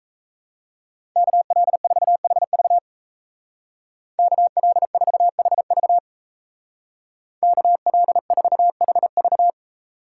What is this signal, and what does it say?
Morse code, heard as a keyed tone, about 33 wpm, 700 Hz. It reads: KL4HV KL4HV KL4HV
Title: Speed Racing - USA 2x2 Call Signs 28wpm